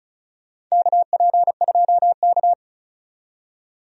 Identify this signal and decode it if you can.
Morse code, heard as a keyed tone, about 36 wpm, 700 Hz. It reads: KP2K
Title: DXCC-100 Call Signs - 4 Characters - 35wpm